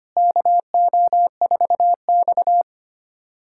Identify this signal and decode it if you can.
Morse code, heard as a keyed tone, about 25 wpm, 700 Hz. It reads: KO4X
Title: USA Callsign DE Callsign - 4 Characters - 25wpm